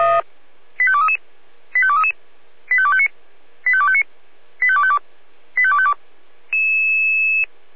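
Five-tone selective call (ZVEI), heard as quick runs of five tones, tones of about 70 ms. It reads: 85233 85233 85259 85259 85252 85252
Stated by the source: ZVEI five-tone selective call 85280, received by SDR.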